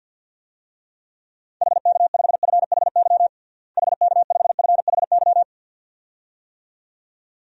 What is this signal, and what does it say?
Morse code, heard as a keyed tone, about 48 wpm, 700 Hz. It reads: HK5FHY HK5FHY
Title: DXCC-100 Call Signs - 6 Characters - Repeated 2x - 50wpm